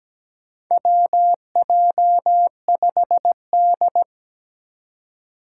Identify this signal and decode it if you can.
Morse code, heard as a keyed tone, about 17 wpm, 700 Hz. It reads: WJ5D
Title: USA Call Signs - 4 Characters - 2x Character Spacing - 17wpm